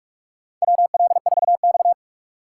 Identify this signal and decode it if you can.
Morse code, heard as a keyed tone, about 41 wpm, 700 Hz. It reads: WL4X
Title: USA Call Signs - 8x Character Spacing - 45wpm